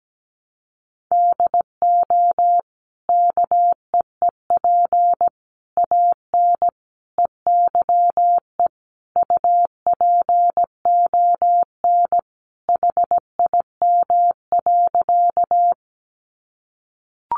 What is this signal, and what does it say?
Morse code, heard as a keyed tone, about 17 wpm, 700 Hz. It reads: DO KEEP AN EYE UPON HIM.
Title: Sentences from Top 1100 Words 17wpm